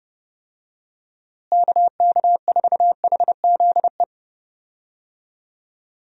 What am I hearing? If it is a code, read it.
Morse code, heard as a keyed tone, about 30 wpm, 700 Hz. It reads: KK4HZE
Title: USA Call Signs - 6 Characters - Repeated 2x - 30wpm